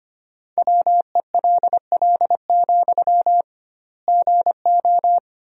Morse code, heard as a keyed tone, about 25 wpm, 700 Hz. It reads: WELL, GO
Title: 2x Word-Spacing - Sentences from Top 200 Words - 25wpm